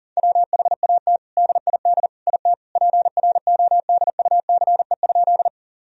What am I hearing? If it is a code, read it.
Morse code, heard as a keyed tone, about 40 wpm, 700 Hz. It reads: WHAT DID IT PRODUCE?
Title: Sentences from Top 1000 Words - Review Part 3 - 40wpm